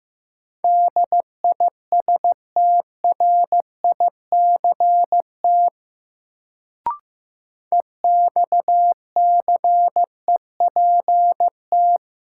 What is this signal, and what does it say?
Morse code, heard as a keyed tone, about 15 wpm, 700 Hz. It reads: DISTRICT EXCEPT
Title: Sentences from Top 1600 Words - New Word Drill - 15wpm